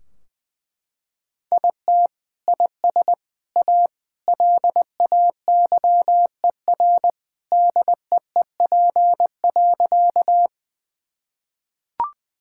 Morse code, heard as a keyed tone, about 20 wpm, 700 Hz. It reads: IT IS A LAYER DEEP.
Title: Sentences from Top 1600 Words 20wpm